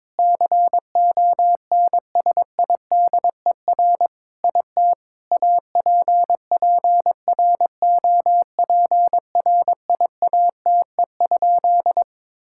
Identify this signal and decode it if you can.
Morse code, heard as a keyed tone, about 22 wpm, 700 Hz. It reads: CONSIDER IT APPROPRIATE?
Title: Sentences from Top 1000 Words - Review Part 5 - 22wpm